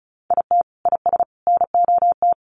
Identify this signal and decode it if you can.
Morse code, heard as a keyed tone, about 34 wpm, 700 Hz. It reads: IT IS NOT
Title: Sets of 3 Words 35wpm